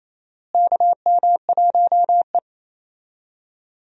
Morse code, heard as a keyed tone, about 28 wpm, 700 Hz. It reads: KM1E
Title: USA Call Signs - 4 Characters - Repeated 2x - 28wpm